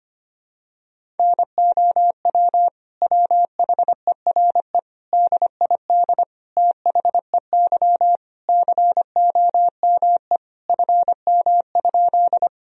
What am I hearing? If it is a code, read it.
Morse code, heard as a keyed tone, about 25 wpm, 700 Hz. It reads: NOW WHERE DID THEY COME FM?
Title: Sentences from Top 200 Words and Abbreviations 25wpm